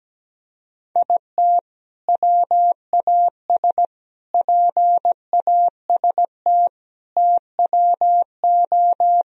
Morse code, heard as a keyed tone, about 17 wpm, 700 Hz. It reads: IT WAS PAST TWO
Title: Sentences from Top 1000 Words - Review Part 3 - 17wpm